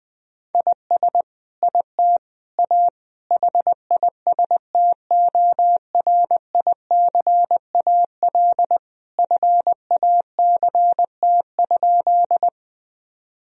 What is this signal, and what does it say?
Morse code, heard as a keyed tone, about 20 wpm, 700 Hz. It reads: IS IT A HISTORICAL FACT?
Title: Sentences from Top 1000 Words - Review Part 4 - 20wpm